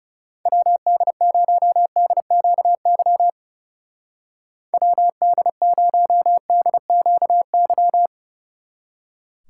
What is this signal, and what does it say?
Morse code, heard as a keyed tone, about 37 wpm, 700 Hz. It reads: WD0DQY WD0DQY
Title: Speed Racing - USA 2x3 Call Signs 30wpm